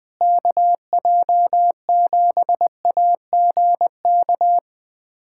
Morse code, heard as a keyed tone, about 20 wpm, 700 Hz. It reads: KJ7AGK
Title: USA Callsign DE Callsign 20wpm